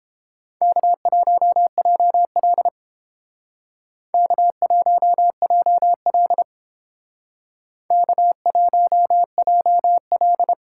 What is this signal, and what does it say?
Morse code, heard as a keyed tone, about 29 wpm, 700 Hz. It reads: K1JL K1JL K1JL